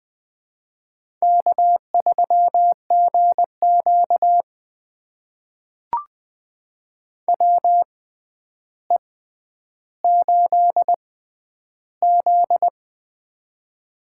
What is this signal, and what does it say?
Morse code, heard as a keyed tone, about 20 wpm, spaced as for 10 wpm, 700 Hz. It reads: K3GQ WE8Z
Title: USA Call Signs - 6x Character Spacing - 20wpm